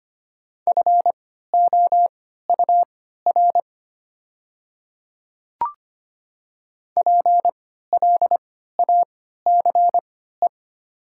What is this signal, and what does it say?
Morse code, heard as a keyed tone, about 25 wpm, spaced as for 14 wpm, 700 Hz. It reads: FOUR PLACE
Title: 3x Character Spacing - Top 300 Words - 25wpm